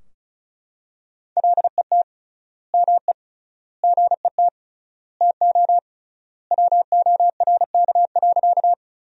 Morse code, heard as a keyed tone, about 36 wpm, 700 Hz. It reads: LET ME GET TO WORK.
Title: 3x Word-Spacing - Sentences from Top 300 Words - 35wpm